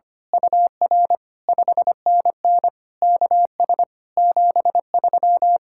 Morse code, heard as a keyed tone, about 24 wpm, 700 Hz. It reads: UR 5NN KS 73
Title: Speed-Racing Common QSO Phrases - 25wpm